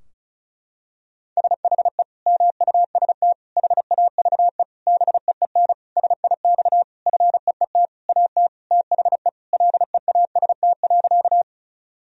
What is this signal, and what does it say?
Morse code, heard as a keyed tone, about 34 wpm, 700 Hz. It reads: SHE MUST HAVE BEEN SIX FEET AT THE LEAST.
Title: Sentences from Top 1000 Words - Review Part 2 - 35wpm